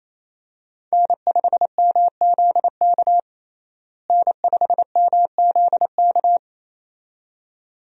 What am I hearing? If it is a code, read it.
Morse code, heard as a keyed tone, about 28 wpm, 700 Hz. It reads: N5MZK N5MZK
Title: USA Call Signs - 5 Characters - Repeated 2x - 28wpm